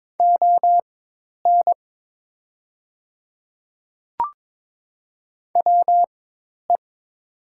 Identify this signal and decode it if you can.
Morse code, heard as a keyed tone, about 22 wpm, spaced as for 10 wpm, 700 Hz. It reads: ON WE
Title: 4x Character Spacing - Top 100 Words - 22wpm